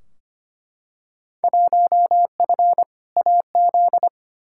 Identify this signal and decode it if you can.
Morse code, heard as a keyed tone, about 25 wpm, 700 Hz. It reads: 1F AZ
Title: Field Day Exchanges - 25wpm